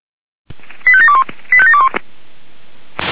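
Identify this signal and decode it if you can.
Five-tone selective call (ZVEI), heard as quick runs of five tones, tones of about 70 ms. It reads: 85721 85721
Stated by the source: ZVEI five-tone selective call 85730 received by SDR.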